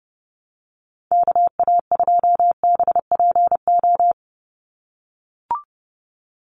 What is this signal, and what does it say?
Morse code, heard as a keyed tone, about 30 wpm, 700 Hz. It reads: KA2BPO
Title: USA Call Signs 30wpm